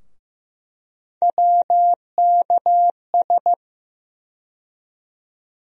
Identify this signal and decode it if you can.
Morse code, heard as a keyed tone, about 15 wpm, 700 Hz. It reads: WKS
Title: USA Call Sign Postfixes - 3 Letters 15wpm